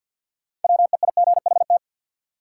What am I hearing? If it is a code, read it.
Morse code, heard as a keyed tone, about 53 wpm, 700 Hz. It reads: WEIGHT
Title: Sentences from Top 800 Words - New Word Drill - 50wpm